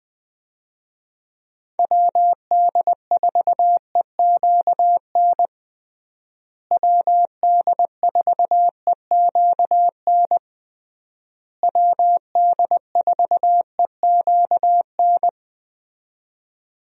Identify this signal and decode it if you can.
Morse code, heard as a keyed tone, about 21 wpm, 700 Hz. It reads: WD4EQN WD4EQN WD4EQN
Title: USA Call Signs - 6 Characters - Repeated 3x - 20wpm